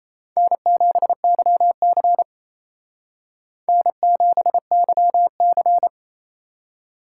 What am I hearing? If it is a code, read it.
Morse code, heard as a keyed tone, about 34 wpm, 700 Hz. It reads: N7YC N7YC
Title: Speed Racing - USA Call Signs 28wpm